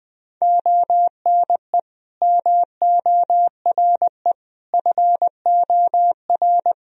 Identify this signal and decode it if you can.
Morse code, heard as a keyed tone, about 21 wpm, 700 Hz. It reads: ONE MORE FOR
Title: Sentences from Top 1000 Words - Review Part 2 - 20wpm